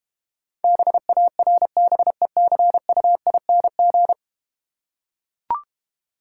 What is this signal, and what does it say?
Morse code, heard as a keyed tone, about 32 wpm, 690 Hz. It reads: BARBECUING